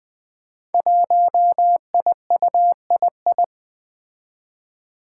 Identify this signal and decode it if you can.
Morse code, heard as a keyed tone, about 20 wpm, 700 Hz. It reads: 1IUII
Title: Five Challenging Letter-Number Combinations 20wpm